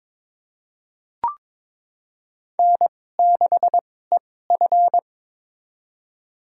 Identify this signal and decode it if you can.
Morse code, heard as a keyed tone, about 22 wpm, spaced as for 16 wpm, 700 Hz. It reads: N6EF
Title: USA Call Signs - 4 Characters - 2x Character Spacing - 22wpm